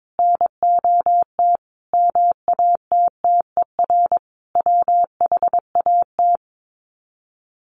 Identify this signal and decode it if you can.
Morse code, heard as a keyed tone, about 22 wpm, 700 Hz. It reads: NOT MATTER WHAT
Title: Sets of 5 Words Encore 22wpm